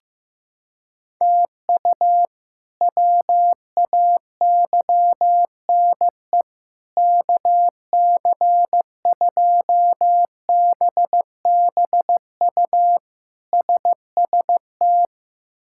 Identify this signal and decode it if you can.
Morse code, heard as a keyed tone, about 15 wpm, 700 Hz. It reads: TU WAYNE KC2BBU SST
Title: K1USN SST QSO Practice - 15wpm